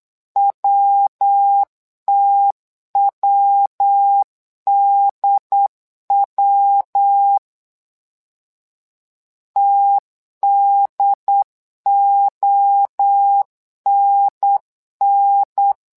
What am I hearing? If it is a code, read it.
Morse code, heard as a keyed tone, about 8 wpm, 800 Hz. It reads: WTWDW TDONN
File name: LEZIONI / 3 MWRO / 3 MWRO 8wpm 2606